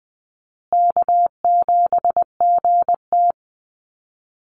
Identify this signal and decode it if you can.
Morse code, heard as a keyed tone, about 20 wpm, 700 Hz. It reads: K7GT
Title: Call Signs Repeated 2x - 20wpm